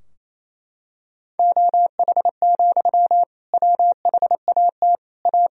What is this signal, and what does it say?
Morse code, heard as a keyed tone, about 28 wpm, 700 Hz. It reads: OH, WHAT A
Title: Sentences from Top 1000 Words - Review Part 2 - 28wpm